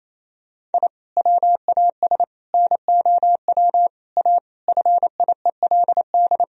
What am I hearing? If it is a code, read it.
Morse code, heard as a keyed tone, about 28 wpm, 700 Hz. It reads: I WAS NOW A FIELD
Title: Sentences from Top 1000 Words - Review Part 2 - 28wpm